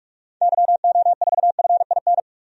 Morse code, heard as a keyed tone, about 46 wpm, 700 Hz. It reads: YO4FIN
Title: DXCC-100 Call Signs - 6 Characters - 45wpm